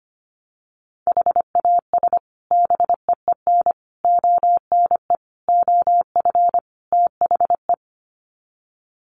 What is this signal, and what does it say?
Morse code, heard as a keyed tone, about 25 wpm, 700 Hz. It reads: HAS BEEN ONE OF THE
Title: Sets of 5 Words Encore 25wpm